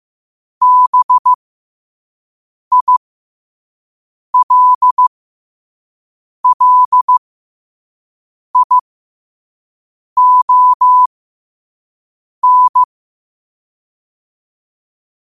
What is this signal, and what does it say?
Morse code, heard as a keyed tone, about 15 wpm, spaced as for 5 wpm, 1000 Hz. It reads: BILLION
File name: qotdfiles / 5wpm